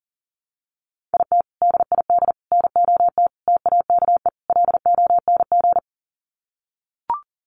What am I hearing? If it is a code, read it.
Morse code, heard as a keyed tone, about 37 wpm, 700 Hz. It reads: IT DID NOT TAKE LONG
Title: Sets of 5 Words Encore2 40wpm